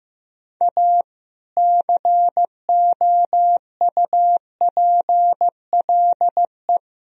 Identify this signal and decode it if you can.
Morse code, heard as a keyed tone, about 15 wpm, 700 Hz. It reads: A COUPLE
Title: Speed-Racing - Sets of 5 Words 15wpm